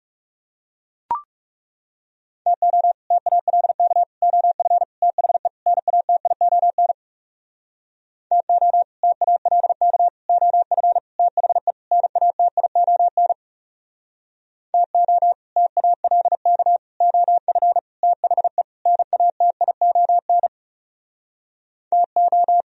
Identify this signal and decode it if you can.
Morse code, heard as a keyed tone, about 40 wpm, 700 Hz. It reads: TO TALK OF THE NATION TO TALK OF THE NATION TO TALK OF THE NATION TO